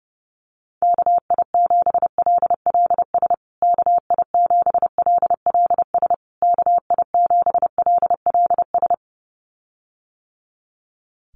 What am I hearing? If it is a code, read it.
Morse code, heard as a keyed tone, about 30 wpm, 700 Hz. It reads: KI7LLS KI7LLS KI7LLS
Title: USA Call Signs Repeated 3x 30wpm